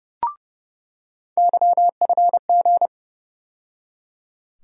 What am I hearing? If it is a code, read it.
Morse code, heard as a keyed tone, about 30 wpm, 700 Hz. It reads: YFG
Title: USA Call Sign Postfixes - 3 Letters 30wpm